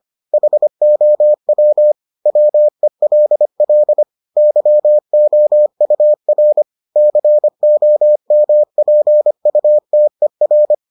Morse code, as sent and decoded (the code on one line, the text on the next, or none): .... --- .-- / .-- . .-.. .-.. / -.-- --- ..- .-. / -.-. --- -- .--. ..- - . .-.
HOW WELL YOUR COMPUTER